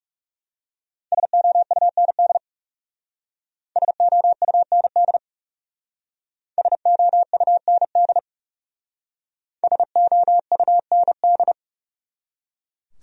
Morse code, heard as a keyed tone, about 38 wpm, 700 Hz. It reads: SOUND SOUND SOUND SOUND